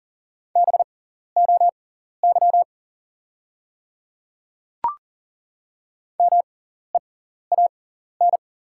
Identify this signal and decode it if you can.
Morse code, heard as a keyed tone, about 37 wpm, spaced as for 14 wpm, 700 Hz. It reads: BOY MEAN
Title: 6x Character Spacing - Top 200 Words - 40wpm